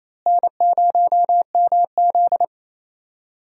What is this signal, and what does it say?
Morse code, heard as a keyed tone, about 28 wpm, 700 Hz. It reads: N0MZ
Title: USA Call Signs - 4 Characters - Repeated 2x - 28wpm